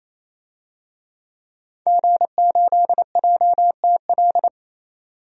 Morse code, heard as a keyed tone, about 28 wpm, 700 Hz. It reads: G8JTL
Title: DXCC-100 Call Signs - 5 Characters - 28wpm